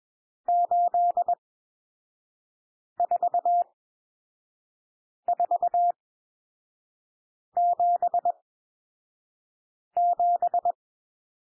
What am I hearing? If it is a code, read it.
Morse code, heard as a keyed tone, about 21 wpm, spaced as for 5 wpm, 700 Hz. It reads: 84477
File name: cwmp3 / N1 SOUND NUMBERS 21 wpm